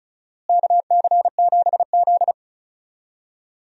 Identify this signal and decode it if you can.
Morse code, heard as a keyed tone, about 34 wpm, 700 Hz. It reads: KC7Z
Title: USA Call Signs - 4 Characters - Repeated 2x - 35wpm